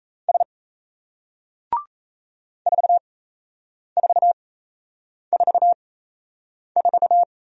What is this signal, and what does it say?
Morse code, heard as a keyed tone, about 38 wpm, spaced as for 8 wpm, 700 Hz. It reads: S 4444